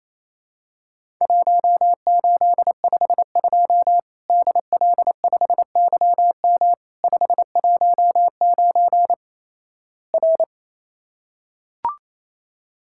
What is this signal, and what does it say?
Morse code, heard as a keyed tone, about 28 wpm, 700 Hz. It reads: 1852 DL5YM 519 R
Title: Worked All Europe DX Contest - QTC Exchange - 28wpm